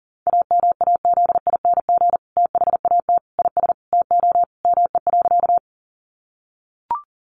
Morse code, heard as a keyed tone, about 37 wpm, 700 Hz. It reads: AMAZING THAT IS TO ME.
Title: Sentences from Top 1500 Words 40wpm